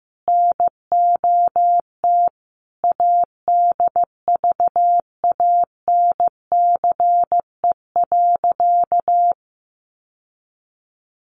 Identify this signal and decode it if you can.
Morse code, heard as a keyed tone, about 15 wpm, 700 Hz. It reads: NOT ADVANCE.
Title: Sentences from Top 1400 Words 15wpm